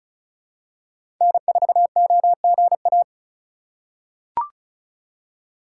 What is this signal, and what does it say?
Morse code, heard as a keyed tone, about 36 wpm, 700 Hz. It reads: N4OGA
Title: USA Call Signs - 5 Characters - 3x Character Spacing - 35wpm